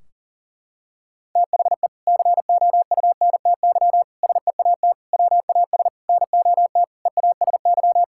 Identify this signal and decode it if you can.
Morse code, heard as a keyed tone, about 40 wpm, 700 Hz. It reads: THE COUNTY SEAT WAS NOT EASY
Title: Sentences from Top 1600 Words 40wpm